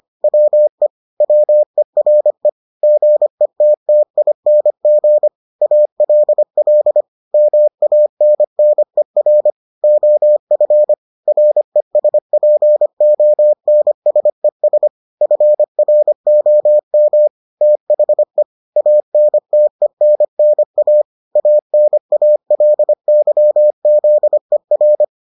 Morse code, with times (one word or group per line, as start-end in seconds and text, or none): 0.2-0.9 WE
1.2-2.5 WERE
2.8-5.3 GETTING
5.6-7.0 ALL
7.3-9.5 MANNER
9.8-10.9 OF
11.3-14.9 RESPONSES
15.2-17.3 FROM
17.6-18.4 THE
18.8-21.0 ANTENNA
21.4-25.0 ANALYZER